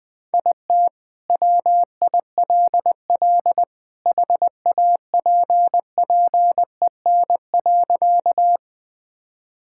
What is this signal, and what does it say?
Morse code, heard as a keyed tone, about 20 wpm, 700 Hz. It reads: IT WILL HAPPEN.
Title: Sentences with 2-Word Phrases - 20wpm